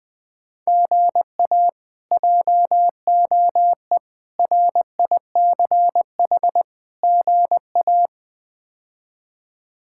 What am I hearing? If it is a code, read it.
Morse code, heard as a keyed tone, about 21 wpm, 700 Hz. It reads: GA JOE RICH GA
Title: K1USN SST QSO Practice - 20wpm